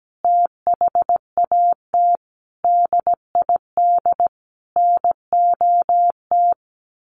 Morse code, heard as a keyed tone, about 17 wpm, 700 Hz. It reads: THAT DID NOT
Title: Sentences from Top 800 Words 17wpm